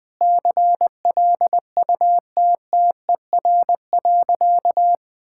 Morse code, heard as a keyed tone, about 20 wpm, 700 Hz. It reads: CLUTTER.
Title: Sentences with 5-Word Phrases - 20wpm